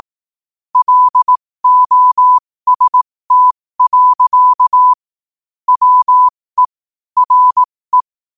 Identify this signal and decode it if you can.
Morse code, heard as a keyed tone, about 18 wpm, spaced as for 15 wpm, 1000 Hz. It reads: LOST. WE RE
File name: qotdfiles / 15wpm